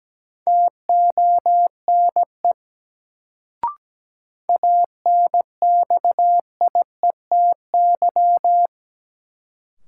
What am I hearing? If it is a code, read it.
Morse code, heard as a keyed tone, about 17 wpm, 700 Hz. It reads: TONE ANXIETY
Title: Sentences from Top 1600 Words - New Word Drill - 17wpm